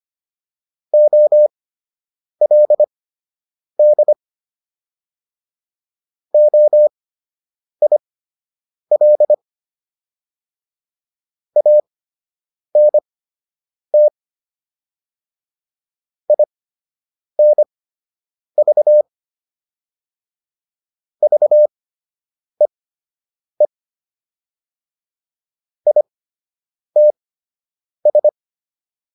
Morse code, heard as a keyed tone, about 25 wpm, spaced as for 8 wpm, 600 Hz. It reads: OLD OIL ANT INV VEE ITS